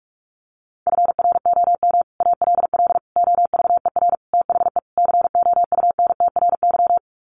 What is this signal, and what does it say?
Morse code, heard as a keyed tone, about 44 wpm, 700 Hz. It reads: FROM ALL OVER THE COUNTRY
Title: Sets of 5 Words Encore 45wpm